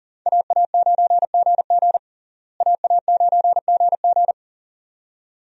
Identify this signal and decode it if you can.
Morse code, heard as a keyed tone, about 37 wpm, 700 Hz. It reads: AA9GG AA9GG
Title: USA Call Signs - 5 Characters - Repeated 2x - 40wpm